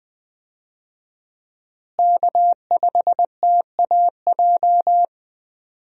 Morse code, heard as a keyed tone, about 21 wpm, 700 Hz. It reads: K5TAJ
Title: USA Call Signs - 5 Characters - Repeated 3x - 20wpm